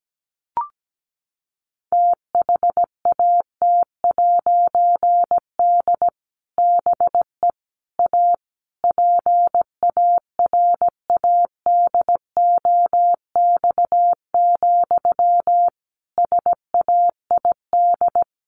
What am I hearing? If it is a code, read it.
Morse code, heard as a keyed tone, about 17 wpm, 700 Hz. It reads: THAT'D BE A PARADOX, SAID
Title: A Sound Of Thunder 17wpm